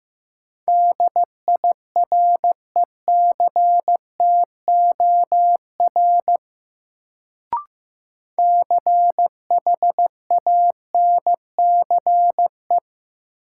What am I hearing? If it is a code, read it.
Morse code, heard as a keyed tone, about 15 wpm, 700 Hz. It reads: DIRECTOR CHANCE
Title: Sets of 3 Words - New Word Drill - 15wpm